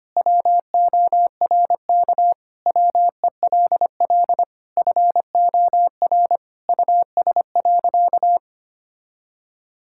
Morse code, heard as a keyed tone, about 25 wpm, 700 Hz. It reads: WORK WELL FOR US.
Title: Sentences from Top 1000 Words - Review Part 6 - 25wpm